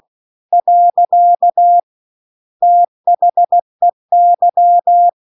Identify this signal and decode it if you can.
Morse code, heard as a keyed tone, about 16 wpm, 700 Hz. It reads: . THEY